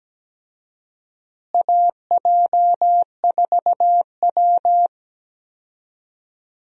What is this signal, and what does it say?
Morse code, heard as a keyed tone, about 17 wpm, 700 Hz. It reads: AJ4W
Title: USA Call Signs - 4 Characters - Repeated 2x - 17wpm